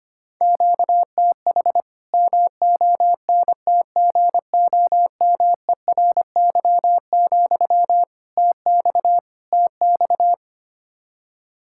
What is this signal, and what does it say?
Morse code, heard as a keyed tone, about 25 wpm, 700 Hz. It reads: QTH MONTGOMERY, TX TX
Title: K1USN SST QSO Practice 25wpm